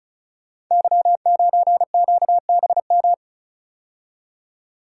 Morse code, heard as a keyed tone, about 36 wpm, 700 Hz. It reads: Y9QBM
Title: Code Groups - 5 Letter:Numbers - 35wpm